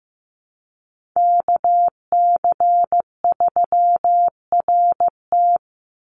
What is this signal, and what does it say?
Morse code, heard as a keyed tone, about 15 wpm, 700 Hz. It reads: KC3RT
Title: NA SOTA Chaser Callsigns 15wpm